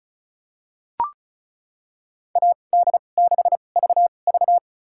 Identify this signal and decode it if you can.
Morse code, heard as a keyed tone, about 36 wpm, spaced as for 25 wpm, 700 Hz. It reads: AD6VV
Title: USA Call Signs - 5 Characters - 2x Character Spacing - 35wpm